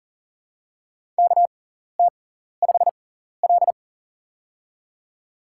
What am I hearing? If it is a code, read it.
Morse code, heard as a keyed tone, about 40 wpm, spaced as for 14 wpm, 700 Hz. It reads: KT5L
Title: USA Call Signs - 6x Character Spacing - 40wpm